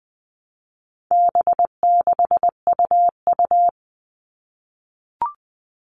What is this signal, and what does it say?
Morse code, heard as a keyed tone, about 20 wpm, 700 Hz. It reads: B6UU
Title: All Cut Numbers - 4-Digit 20wpm